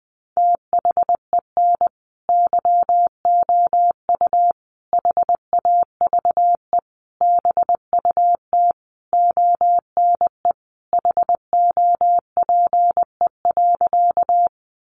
Morse code, heard as a keyed tone, about 19 wpm, 700 Hz. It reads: THEN YOU HAVE BUT ONE HOPE.
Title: Sentences from Top 1000 Words 20wpm